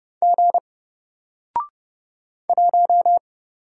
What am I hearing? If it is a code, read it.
Morse code, heard as a keyed tone, about 30 wpm, 700 Hz. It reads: G 1
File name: Single-Letter-Number-30wpm